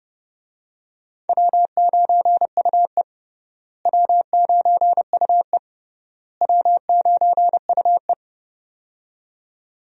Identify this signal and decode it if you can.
Morse code, heard as a keyed tone, about 31 wpm, 700 Hz. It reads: W9UE W9UE W9UE
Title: USA Call Signs - 4 Characters - Repeated 3x - 30wpm